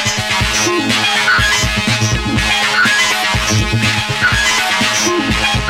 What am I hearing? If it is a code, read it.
Emergency Alert System (SAME) tones, heard as harsh buzzing bursts. It is heard from 1.2 to 4.6 s.